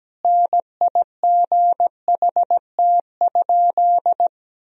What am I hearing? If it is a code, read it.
Morse code, heard as a keyed tone, about 17 wpm, 700 Hz. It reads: NIGHT?
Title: Sentences from Top 1000 Words - Review Part 1 - 17wpm